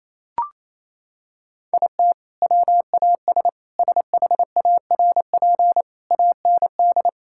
Morse code, heard as a keyed tone, about 28 wpm, 700 Hz. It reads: IT WAS SHARP AND
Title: Sentences from Top 1600 Words 28wpm